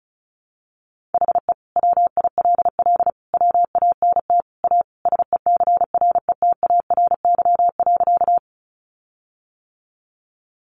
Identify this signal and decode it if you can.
Morse code, heard as a keyed tone, about 34 wpm, 700 Hz. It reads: HE WILL WANT A SECRETARY.
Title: Sentences from Top 1300 Words 35wpm